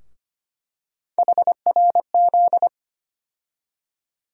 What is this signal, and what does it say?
Morse code, heard as a keyed tone, about 25 wpm, 700 Hz. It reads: HRZ
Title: USA Call Sign Postfixes - 3 Letters 25wpm